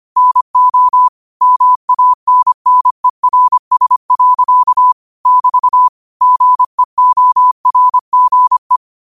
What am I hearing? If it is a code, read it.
Morse code, heard as a keyed tone, about 25 wpm, 1000 Hz. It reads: NO MANNERS. <BT> GEORGE